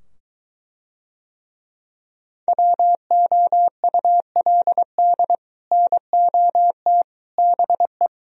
Morse code, heard as a keyed tone, about 23 wpm, 700 Hz. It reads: WOULD NOT BE